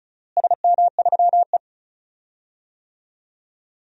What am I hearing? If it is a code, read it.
Morse code, heard as a keyed tone, about 34 wpm, 700 Hz. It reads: SM3E
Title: DXCC-100 Call Signs - 4 Characters - Repeated 3x - 35wpm